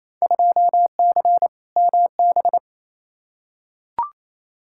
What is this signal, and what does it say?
Morse code, heard as a keyed tone, about 28 wpm, 700 Hz. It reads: 2C MB
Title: Field Day Exchanges Encore - 28wpm